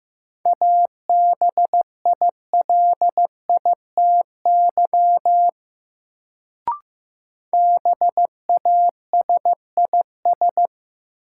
Morse code, heard as a keyed tone, about 15 wpm, 700 Hz. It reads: ABILITY BASIS
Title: Sentences from Top 1200 Words - New Word Drill Alt - 15wpm